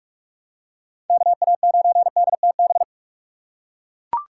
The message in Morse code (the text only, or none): KA9DTB